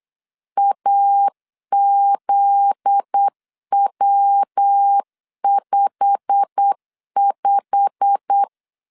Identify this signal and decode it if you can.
Morse code, heard as a keyed tone, about 8 wpm, 800 Hz. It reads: AZW55
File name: LEZIONI / 8 YJ45 / 8 YJ45 8wpm 3214